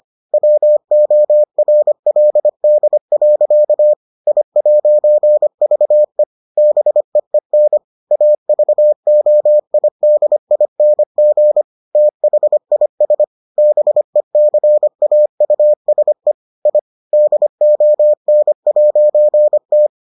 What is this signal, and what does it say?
Morse code, heard as a keyed tone, about 25 wpm, 600 Hz. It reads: WORLD. I'VE BEEN AVOIDING THIS BECAUSE I DON'T